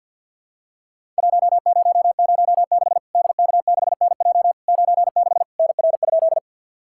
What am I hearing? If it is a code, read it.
Morse code, heard as a keyed tone, about 50 wpm, 700 Hz. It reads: 1006 DK6NJ 96 NR?
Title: Worked All Europe DX Contest - QTC Exchange with Repeats - 50wpm